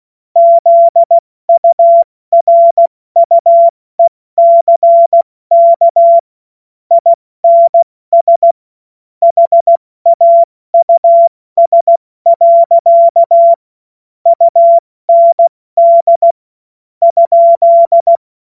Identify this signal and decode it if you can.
Morse code, heard as a keyed tone, about 16 wpm, 680 Hz. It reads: ZURUECK INS HAUS. UND ?
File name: Morsekurs / Witze / Witz 29 April 2025 70 F80